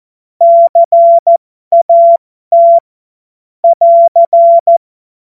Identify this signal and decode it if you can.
Morse code, heard as a keyed tone, about 14 wpm, spaced as for 12 wpm, 680 Hz. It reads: CAT <AR>